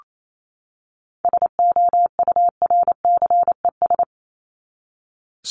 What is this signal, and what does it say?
Morse code, heard as a keyed tone, about 28 wpm, 700 Hz. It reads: SOURCES
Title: ES:S Suffix - Top 100 Words with Base Words 28wpm